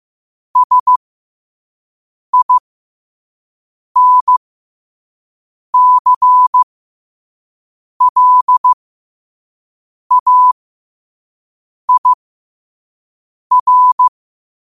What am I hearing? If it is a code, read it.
Morse code, heard as a keyed tone, about 15 wpm, spaced as for 5 wpm, 1000 Hz. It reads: SINCLAIR
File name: qotdfiles / 5wpm